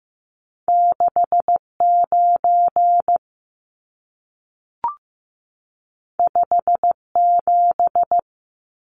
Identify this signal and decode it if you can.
Morse code, heard as a keyed tone, about 15 wpm, 700 Hz. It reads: 6 9 5 7
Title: Numbers - 2-Digit 15wpm